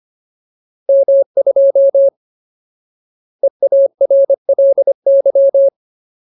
Morse code, heard as a keyed tone, about 25 wpm, 550 Hz. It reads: M2 EARLY
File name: Ultimate Word List / 25WPM Ultimate